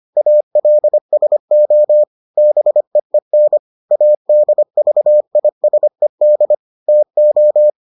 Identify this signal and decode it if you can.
Morse code, heard as a keyed tone, about 25 wpm, 600 Hz. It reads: ALSO BEEN ADVISED TO